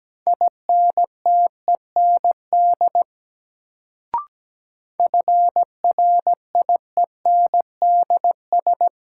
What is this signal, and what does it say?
Morse code, heard as a keyed tone, about 17 wpm, 700 Hz. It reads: INTEND FRIENDS
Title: Sentences from Top 1600 Words - New Word Drill Alt - 17wpm